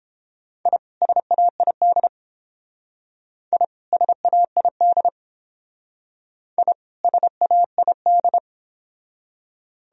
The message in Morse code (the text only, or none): I SAID I SAID I SAID